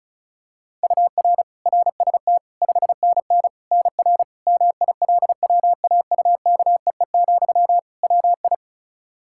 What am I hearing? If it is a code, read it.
Morse code, heard as a keyed tone, about 36 wpm, 700 Hz. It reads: UR RST 5NN NR MILWAUKEE, WI
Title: Practice QSO - Exchange 1 - Beginning - 35wpm